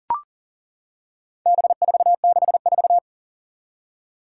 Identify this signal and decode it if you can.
Morse code, heard as a keyed tone, about 40 wpm, 700 Hz. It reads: B464
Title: Four Challenging Letter-Number Combinations 40wpm